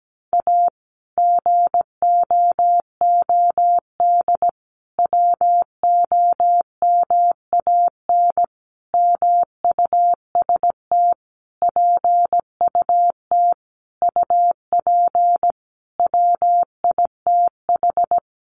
Morse code, heard as a keyed tone, about 17 wpm, 700 Hz. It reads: A GOOD WOMAN MUST PUT UP WITH
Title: Sentences from Top 1100 Words 17wpm